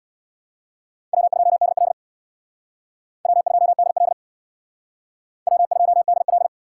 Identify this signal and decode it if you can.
Morse code, heard as a keyed tone, about 61 wpm, 700 Hz. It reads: W2DL W2DL W2DL
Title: Speed Racing - USA 1x2 Call Signs 50wpm